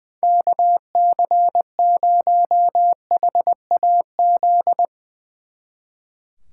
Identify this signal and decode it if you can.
Morse code, heard as a keyed tone, about 21 wpm, 700 Hz. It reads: KC0HAZ
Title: Speed Racing - USA 2x3 Call Signs 20wpm